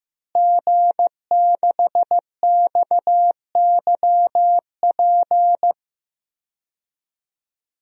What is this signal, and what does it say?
Morse code, heard as a keyed tone, about 15 wpm, 700 Hz. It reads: G6XYP
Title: DXCC-100 Call Signs - 5 Characters - Repeated 3x - 15wpm